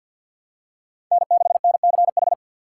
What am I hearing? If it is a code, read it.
Morse code, heard as a keyed tone, about 47 wpm, 700 Hz. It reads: N6NCH
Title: USA Call Signs - 6x Character Spacing - 50wpm